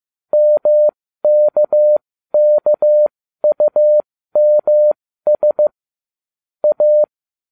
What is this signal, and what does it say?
Morse code, heard as a keyed tone, about 15 wpm, spaced as for 13 wpm, 600 Hz. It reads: MKKUMS A